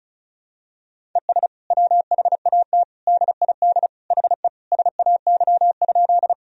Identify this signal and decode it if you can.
Morse code, heard as a keyed tone, about 36 wpm, 700 Hz. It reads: ES WHAT DID HE SAY?
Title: Sentences from Top 200 Words and Abbreviations 35wpm